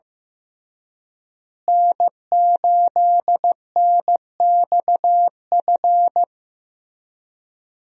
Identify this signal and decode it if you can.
Morse code, heard as a keyed tone, about 16 wpm, 700 Hz. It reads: N8NXF
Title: USA Call Signs - 5 Characters - Repeated 2x - 15wpm